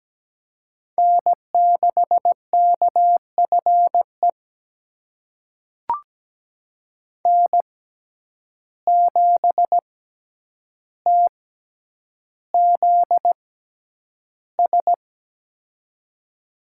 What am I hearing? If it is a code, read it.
Morse code, heard as a keyed tone, about 17 wpm, spaced as for 9 wpm, 700 Hz. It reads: N6KFE N7TZS
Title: USA Call Signs - 5 Characters - 6x Character Spacing - 17wpm